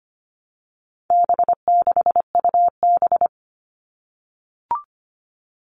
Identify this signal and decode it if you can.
Morse code, heard as a keyed tone, about 25 wpm, 700 Hz. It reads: B6UB
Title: All Cut Numbers - 4-Digit 25wpm